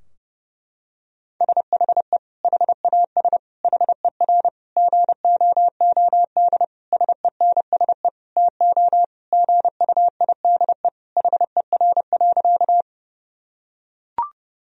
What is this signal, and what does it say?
Morse code, heard as a keyed tone, about 30 wpm, 700 Hz. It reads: SHE HAS HER GOOD SENSE TO GUIDE HER.
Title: Sentences from Top 1000 Words - Review Part 4 - 30wpm